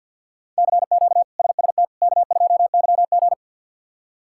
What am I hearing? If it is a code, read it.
Morse code, heard as a keyed tone, about 53 wpm, 700 Hz. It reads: CQ SST K2YG
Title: K1USN SST QSO Practice - 50wpm